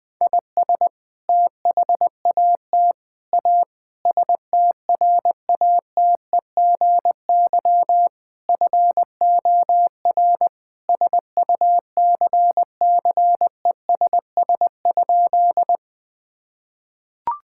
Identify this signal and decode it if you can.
Morse code, heard as a keyed tone, about 20 wpm, 700 Hz. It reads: IS THAT A STRATEGY FOR SUCCESS?
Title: Sentences from Top 1000 Words - Review Part 5 - 20wpm